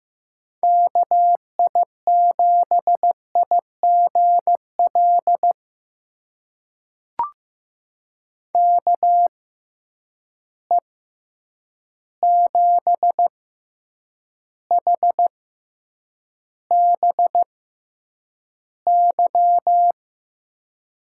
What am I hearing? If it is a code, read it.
Morse code, heard as a keyed tone, about 15 wpm, spaced as for 8 wpm, 700 Hz. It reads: KI7IGL KE7HBY
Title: USA Call Signs - 6 Characters - 6x Character Spacing - 15wpm